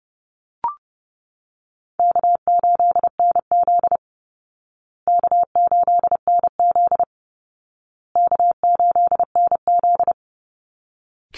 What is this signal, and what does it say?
Morse code, heard as a keyed tone, about 30 wpm, 700 Hz. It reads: K8NZ K8NZ K8NZ
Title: Easy Call Signs Repeated 3x - 30wpm